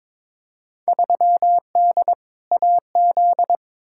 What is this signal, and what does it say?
Morse code, heard as a keyed tone, about 21 wpm, 700 Hz. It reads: 3D AZ
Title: Field Day Exchanges - 22wpm